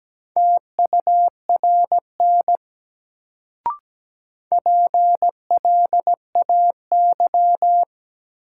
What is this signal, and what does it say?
Morse code, heard as a keyed tone, about 17 wpm, 700 Hz. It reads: TURN PLAY
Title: Sentences from Top 200 Words - New Word Drill - 17wpm